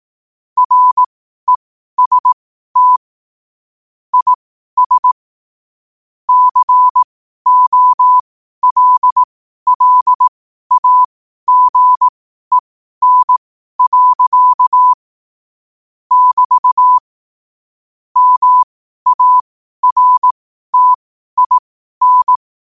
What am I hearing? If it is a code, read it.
Morse code, heard as a keyed tone, about 18 wpm, spaced as for 12 wpm, 1000 Hz. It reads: REST IS COLLAGEN. <BT> MARTIN
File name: qotdfiles / 12wpm